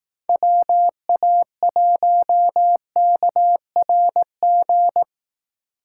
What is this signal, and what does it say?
Morse code, heard as a keyed tone, about 18 wpm, 700 Hz. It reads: WA1KRG
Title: Speed Racing - USA 2x3 Call Signs 15wpm